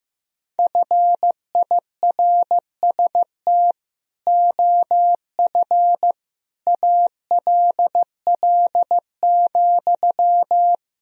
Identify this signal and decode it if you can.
Morse code, heard as a keyed tone, about 16 wpm, 700 Hz. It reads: FIRST OF ALL,
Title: Sentences from Top 1000 Words - Review Part 4 - 15wpm